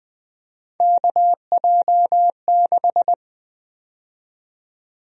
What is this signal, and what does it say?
Morse code, heard as a keyed tone, about 20 wpm, 700 Hz. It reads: KJ6
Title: USA Call Sign Prefixes - 20wpm